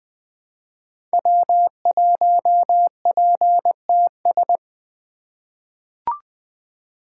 Morse code, heard as a keyed tone, about 20 wpm, 700 Hz. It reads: W1PTS
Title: 2023 SOTA Activator Call Signs - 20wpm